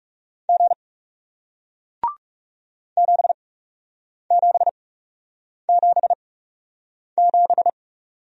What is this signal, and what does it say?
Morse code, heard as a keyed tone, about 41 wpm, spaced as for 8 wpm, 700 Hz. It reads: G 7777